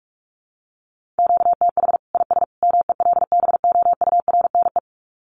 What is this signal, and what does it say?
Morse code, heard as a keyed tone, about 44 wpm, 700 Hz. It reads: QTH IS MELBOURNE
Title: Common QSO Phrases 45wpm